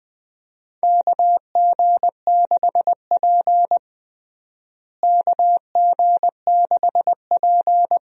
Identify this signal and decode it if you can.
Morse code, heard as a keyed tone, about 21 wpm, 700 Hz. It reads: KG6P KG6P
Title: USA Call Signs - 4 Characters - Repeated 2x - 20wpm